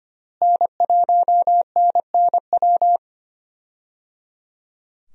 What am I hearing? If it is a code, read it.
Morse code, heard as a keyed tone, about 25 wpm, 700 Hz. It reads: N1NNW
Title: USA Call Signs - 5 Characters - Repeated 3x - 25wpm